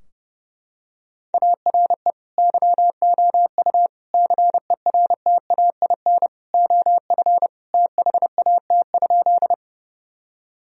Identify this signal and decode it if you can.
Morse code, heard as a keyed tone, about 30 wpm, 700 Hz. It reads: ARE YOU CERTAIN OF THAT?
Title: Sentences from Top 1000 Words - Review Part 2 - 30wpm